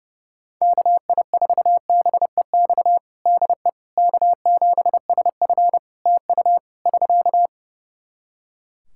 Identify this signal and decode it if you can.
Morse code, heard as a keyed tone, about 32 wpm, 700 Hz. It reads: KI4BEX DE K7SF TU <SK>
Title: Practice QSO - Exchange 4 - Ending - 30wpm